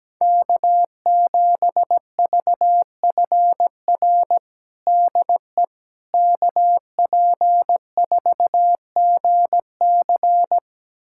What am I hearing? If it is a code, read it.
Morse code, heard as a keyed tone, about 17 wpm, 700 Hz. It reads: K7VFR DE KP4GC
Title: USA Callsign DE Callsign - 5 Characters - 17wpm